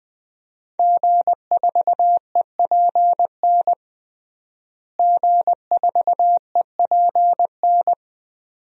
Morse code, heard as a keyed tone, about 20 wpm, 700 Hz. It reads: G4EPN G4EPN
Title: DXCC-100 Call Signs - 5 Characters - Repeated 2x - 20wpm